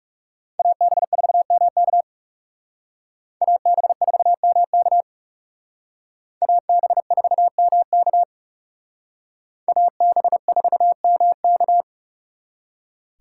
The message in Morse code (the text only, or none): AB4MK AB4MK AB4MK AB4MK